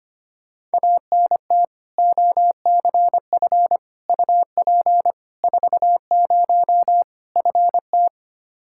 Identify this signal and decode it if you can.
Morse code, heard as a keyed tone, about 25 wpm, 700 Hz. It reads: ANT OCF UP 40 FT
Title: K1USN SST QSO Practice 25wpm